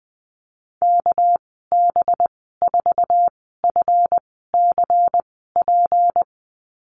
Morse code, heard as a keyed tone, about 20 wpm, spaced as for 14 wpm, 700 Hz. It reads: KB4FCP
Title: 245 Lesson 39g - Letter z - Call Signs 20wpm